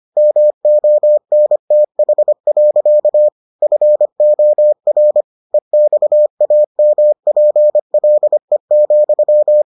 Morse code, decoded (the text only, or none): MONTH. FOR EXAMPLE,